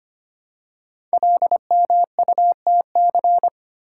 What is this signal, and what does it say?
Morse code, heard as a keyed tone, about 25 wpm, 700 Hz. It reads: LMUTC
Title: Code Groups - 5 Letters - 25wpm